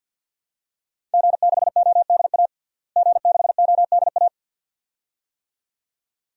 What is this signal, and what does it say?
Morse code, heard as a keyed tone, about 52 wpm, 700 Hz. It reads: G6ODA G6ODA
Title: DXCC-100 Call Signs - 5 Characters - Repeated 2x - 50wpm